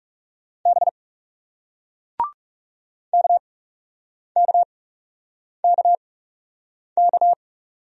Morse code, heard as a keyed tone, about 42 wpm, spaced as for 8 wpm, 700 Hz. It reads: B KKKK